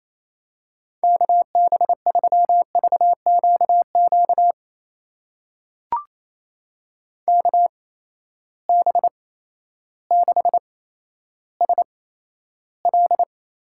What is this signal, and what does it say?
Morse code, heard as a keyed tone, about 28 wpm, spaced as for 13 wpm, 700 Hz. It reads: KB3VQQ KB6SL
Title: USA Call Signs - 6 Characters - 8x Character Spacing - 28wpm